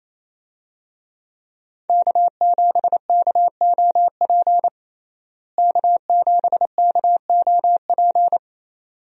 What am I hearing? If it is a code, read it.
Morse code, heard as a keyed tone, about 28 wpm, 700 Hz. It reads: K7KOP K7KOP
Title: USA Call Signs - 5 Characters - Repeated 2x - 28wpm